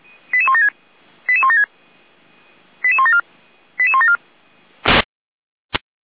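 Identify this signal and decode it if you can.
Five-tone selective call (ZVEI), heard as quick runs of five tones, tones of about 70 ms. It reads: 88176 88176 88174 88174
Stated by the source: ZVEI five-tone call received with an SDR.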